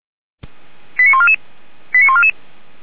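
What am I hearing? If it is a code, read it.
Five-tone selective call (ZVEI), heard as quick runs of five tones, tones of about 70 ms. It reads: 89144 89144
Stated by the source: ZVEI five-tone call 89145 received by SDR.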